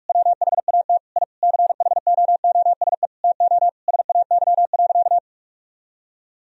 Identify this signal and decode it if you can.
Morse code, heard as a keyed tone, about 46 wpm, 700 Hz. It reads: WHAT I CHOOSE TO SAY.
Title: Sentences from Top 1000 Words - Review Part 4 - 45wpm